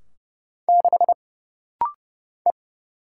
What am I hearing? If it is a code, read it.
Morse code, heard as a keyed tone, about 30 wpm, 700 Hz. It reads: 6 E
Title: Single Letter-Number Alt - Rapid-Fire 30wpm